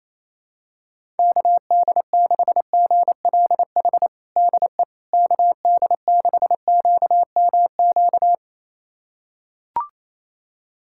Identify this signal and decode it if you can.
Morse code, heard as a keyed tone, about 28 wpm, 700 Hz. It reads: KD6GLH DE KD6QMQ
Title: USA Callsign DE Callsign - 6 Characters - 28wpm